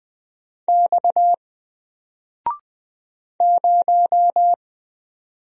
Morse code, heard as a keyed tone, about 20 wpm, 700 Hz. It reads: X 0
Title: Single Letter-Number 20wpm